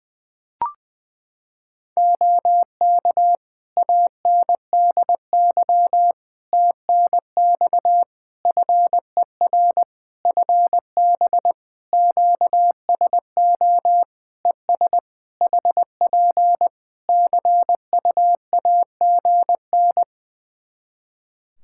Morse code, heard as a keyed tone, about 20 wpm, 700 Hz. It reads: OK ANDY TNX FER FB QSO ES HP CUAGN
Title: K1USN SST QSO Practice 20wpm